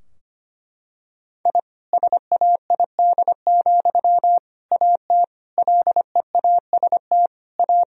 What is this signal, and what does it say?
Morse code, heard as a keyed tone, about 25 wpm, 700 Hz. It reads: I SAID, AT LEAST A
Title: Sentences from Top 1000 Words - Review Part 2 - 25wpm